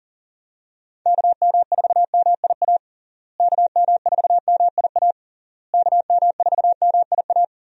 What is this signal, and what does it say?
Morse code, heard as a keyed tone, about 40 wpm, 700 Hz. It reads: KM4MIA KM4MIA KM4MIA
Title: USA Call Signs - 6 Characters - Repeated 3x - 40wpm